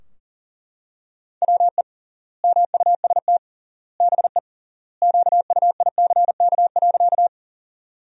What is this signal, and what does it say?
Morse code, heard as a keyed tone, about 40 wpm, 700 Hz. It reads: WE MUST BE QUICK.
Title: 3x Word-Spacing - Sentences from Top 500 Words - 40wpm